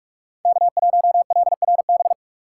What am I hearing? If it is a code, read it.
Morse code, heard as a keyed tone, about 42 wpm, 700 Hz. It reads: K1LRB
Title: USA Call Signs - 5 Characters - 2x Character Spacing - 45wpm